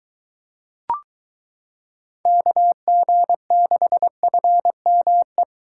Morse code, heard as a keyed tone, about 22 wpm, 700 Hz. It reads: KG6FME